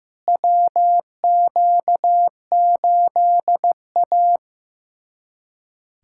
Morse code, heard as a keyed tone, about 16 wpm, 700 Hz. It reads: WQ8A
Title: USA Callsign DE Callsign - 4 Characters - 15wpm